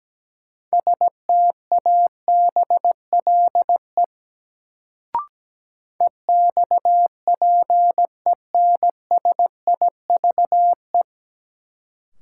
Morse code, heard as a keyed tone, about 17 wpm, 700 Hz. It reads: STABLE EXPENSIVE
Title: Sentences from Top 1500 Words - New Word Drill Alt - 17wpm